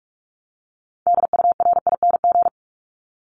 Morse code, heard as a keyed tone, about 44 wpm, 700 Hz. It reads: DURING
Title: Top 500 Words 45wpm